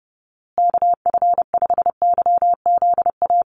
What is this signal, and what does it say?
Morse code, heard as a keyed tone, about 30 wpm, 700 Hz. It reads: KF5YZA
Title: USA Call Signs Repeated 3x 30wpm